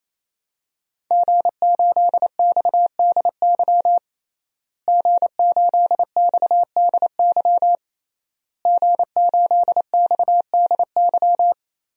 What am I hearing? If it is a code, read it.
Morse code, heard as a keyed tone, about 28 wpm, 700 Hz. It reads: G8XDY G8XDY G8XDY
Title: DXCC-100 Call Signs - 5 Characters - Repeated 3x - 28wpm